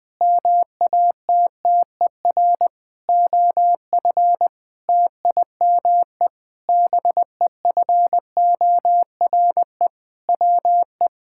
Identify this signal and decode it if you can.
Morse code, heard as a keyed tone, about 20 wpm, 700 Hz. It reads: MATTER OF TIME BEFORE WE
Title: Sentences with 8-Word Phrases - 20wpm